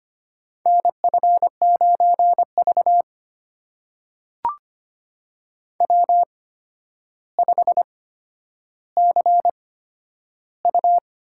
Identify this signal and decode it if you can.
Morse code, heard as a keyed tone, about 25 wpm, spaced as for 11 wpm, 700 Hz. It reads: NF9V W5CU
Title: USA Call Signs - 4 Characters - 8x Character Spacing - 25wpm